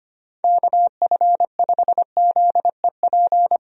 Morse code, heard as a keyed tone, about 25 wpm, 700 Hz. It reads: KF5ZEP
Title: USA Call Signs - 6 Characters - 25wpm